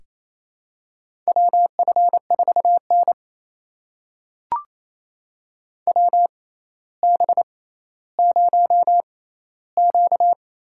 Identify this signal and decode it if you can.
Morse code, heard as a keyed tone, about 28 wpm, spaced as for 14 wpm, 700 Hz. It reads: WF4N WB0Q
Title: USA Call Signs - 6x Character Spacing - 28wpm